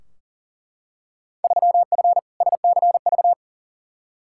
Morse code, heard as a keyed tone, about 40 wpm, 700 Hz. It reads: 3F SCV
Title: Field Day Exchanges - 40wpm